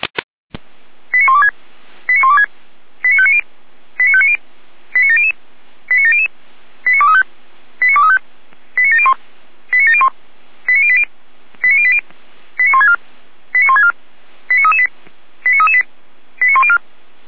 Five-tone selective call (ZVEI), heard as quick runs of five tones, tones of about 70 ms. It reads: 89126 89126 89590 89590 89700 89700 89235 89235 89701 89701 89080 89080 89164 89164 89338 89338 89104